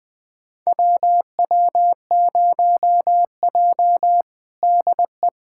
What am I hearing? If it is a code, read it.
Morse code, heard as a keyed tone, about 20 wpm, 700 Hz. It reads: WW0J DE
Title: USA Callsign DE Callsign - 4 Characters - 20wpm